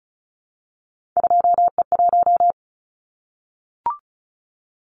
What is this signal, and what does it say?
Morse code, heard as a keyed tone, about 34 wpm, 700 Hz. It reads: 2E1
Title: Callsign Prefixes for Top 25 DXCC - Repeated 3x - 35wpm